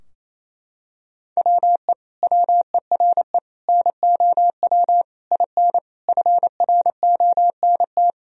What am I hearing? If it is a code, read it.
Morse code, heard as a keyed tone, about 28 wpm, 700 Hz. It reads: WE WERE NOW IN FRONT
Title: Sentences from Top 1000 Words - Review Part 2 - 28wpm